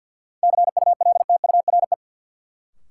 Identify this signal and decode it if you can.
Morse code, heard as a keyed tone, about 52 wpm, 700 Hz. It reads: CULTURE
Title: Sentences from Top 1000 Words - Review Part 6 - New Word Drill - 50wpm